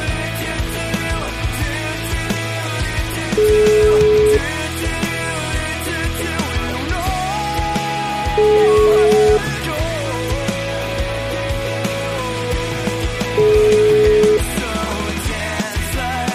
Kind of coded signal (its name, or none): telephone call progress tone